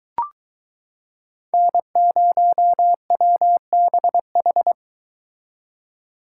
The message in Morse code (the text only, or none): N0WBH